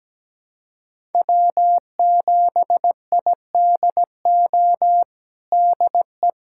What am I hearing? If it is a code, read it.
Morse code, heard as a keyed tone, about 17 wpm, 700 Hz. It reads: W7IDO DE
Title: USA Callsign DE Callsign - 5 Characters - 17wpm